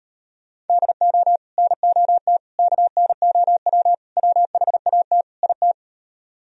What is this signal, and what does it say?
Morse code, heard as a keyed tone, about 33 wpm, 700 Hz. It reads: DO NOT KNOW WHAT IT